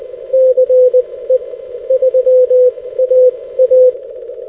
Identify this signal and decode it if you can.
Morse code, heard as a keyed tone, about 20 wpm, spaced as for 15 wpm, 510 Hz. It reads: CE3AA
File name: 6m-ce3aa-beacon-3